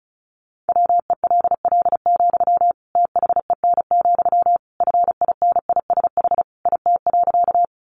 Morse code, heard as a keyed tone, about 34 wpm, 700 Hz. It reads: WELL, THEN, FINISH IT.
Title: Sentences from Top 1200 Words 35wpm